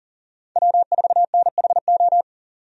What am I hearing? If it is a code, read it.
Morse code, heard as a keyed tone, about 38 wpm, 700 Hz. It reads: W4NHO
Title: USA Call Signs - 5 Characters - 40wpm